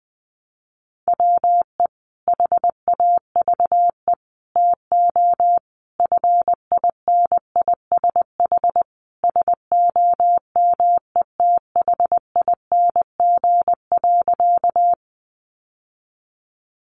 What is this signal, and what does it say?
Morse code, heard as a keyed tone, about 20 wpm, 700 Hz. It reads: WE HAVE TO FINISH SOMETHING.
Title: Sentences from Top 1200 Words 20wpm